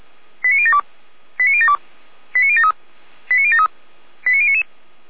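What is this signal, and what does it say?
Five-tone selective call (ZVEI), heard as quick runs of five tones, tones of about 70 ms. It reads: 89072 89072 89073 89073 89099
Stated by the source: ZVEI five-tone call received by SDR.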